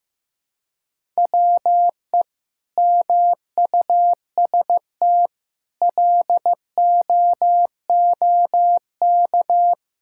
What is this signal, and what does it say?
Morse code, heard as a keyed tone, about 15 wpm, 700 Hz. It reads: WE MUST LOOK
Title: Sentences from Top 1600 Words 15wpm